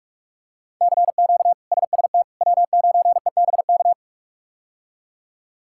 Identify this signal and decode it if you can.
Morse code, heard as a keyed tone, about 47 wpm, 700 Hz. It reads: CQ SST W9EBK
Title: K1USN SST QSO Practice - 45wpm